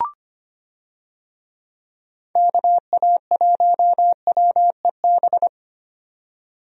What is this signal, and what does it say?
Morse code, heard as a keyed tone, about 25 wpm, 700 Hz. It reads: KA1WEB
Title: USA Call Signs - 6 Characters - 25wpm